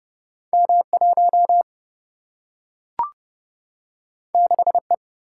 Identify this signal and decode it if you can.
Morse code, heard as a keyed tone, about 30 wpm, 700 Hz. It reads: M1 6E
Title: Single-Letter Single-Number 30wpm